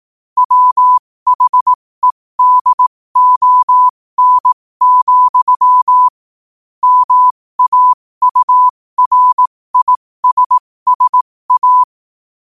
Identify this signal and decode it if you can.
Morse code, heard as a keyed tone, about 18 wpm, spaced as for 15 wpm, 1000 Hz. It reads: WHEDON, MAURISSA
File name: qotdfiles / 15wpm